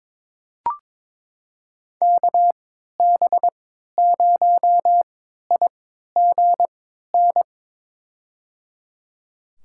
Morse code, heard as a keyed tone, about 22 wpm, spaced as for 13 wpm, 700 Hz. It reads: KB0IGN
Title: USA Call Signs - 6 Characters - 3x Character Spacing - 22wpm